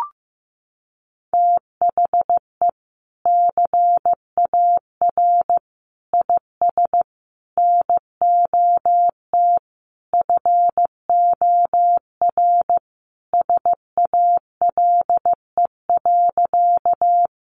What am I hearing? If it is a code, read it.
Morse code, heard as a keyed tone, about 15 wpm, 700 Hz. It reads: THE CAR IS NOT FOR SALE.
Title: Sentences from Top 1400 Words 15wpm